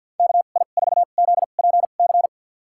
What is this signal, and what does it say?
Morse code, heard as a keyed tone, about 52 wpm, spaced as for 37 wpm, 700 Hz. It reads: KI4ZPC
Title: USA Call Signs - 2x Character Spacing - 50wpm